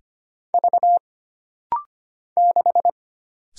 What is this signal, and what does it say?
Morse code, heard as a keyed tone, about 25 wpm, 700 Hz. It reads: V 6
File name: Single Character (Letter:Number:Punctuation:Prosign) - Rapid-Fire - 25wpm